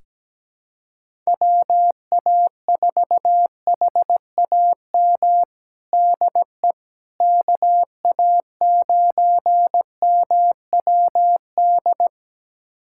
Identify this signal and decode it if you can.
Morse code, heard as a keyed tone, about 17 wpm, 700 Hz. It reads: WA4HAM DE KA9MWD
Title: USA Callsign DE Callsign - 6 Characters - 17wpm